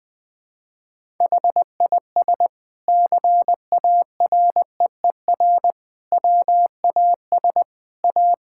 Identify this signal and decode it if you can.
Morse code, heard as a keyed tone, about 20 wpm, 700 Hz. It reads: HIS CAREER WAS A
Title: Sentences from Top 1000 Words - Review Part 6 - 20wpm